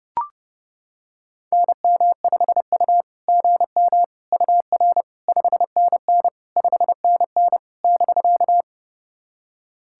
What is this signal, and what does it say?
Morse code, heard as a keyed tone, about 30 wpm, 660 Hz. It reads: NM5U GM UR 5NN 5NN <BK>
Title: SOTA QSO Practice 30wpm